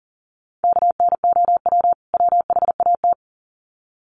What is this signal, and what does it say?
Morse code, heard as a keyed tone, about 38 wpm, 700 Hz. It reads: KNOW WHAT
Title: Sets of 2 Words 40wpm